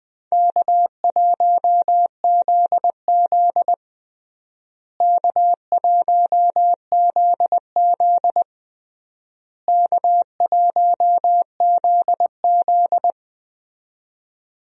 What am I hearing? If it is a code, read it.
Morse code, heard as a keyed tone, about 20 wpm, 700 Hz. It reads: K1ZZ K1ZZ K1ZZ
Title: USA Call Signs - 4 Characters - Repeated 3x - 20wpm